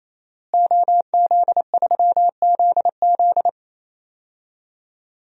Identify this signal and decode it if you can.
Morse code, heard as a keyed tone, about 28 wpm, 700 Hz. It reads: OZ3ZZ
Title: DXCC-100 Call Signs - 5 Characters - Repeated 3x - 28wpm